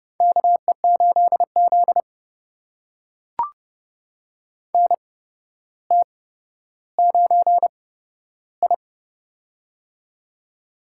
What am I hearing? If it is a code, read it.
Morse code, heard as a keyed tone, about 31 wpm, spaced as for 13 wpm, 700 Hz. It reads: KE8Z NT9I
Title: USA Call Signs - 4 Characters - 8x Character Spacing - 30wpm